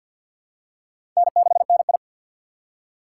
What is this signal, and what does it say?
Morse code, heard as a keyed tone, about 53 wpm, 700 Hz. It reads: N6NI
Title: USA Call Signs - 4 Characters - 3x Character Spacing - 50wpm